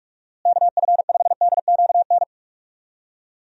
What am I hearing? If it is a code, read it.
Morse code, heard as a keyed tone, about 41 wpm, 700 Hz. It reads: KF5DQN
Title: USA Call Signs - 6 Characters - Repeated 2x - 45wpm